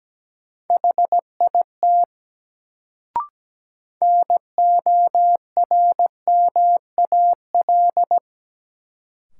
Morse code, heard as a keyed tone, about 17 wpm, 700 Hz. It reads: HIT NORMAL
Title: Sentences from Top 1200 Words - New Word Drill - 17wpm